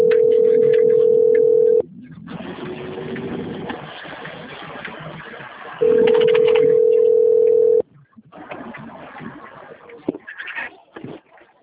Ringback tone (North America), 440 + 480 Hz.
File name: rg-1600-unknown- - - .60191